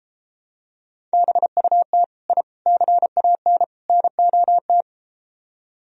Morse code, heard as a keyed tone, about 33 wpm, 700 Hz. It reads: BUT I CAN NOT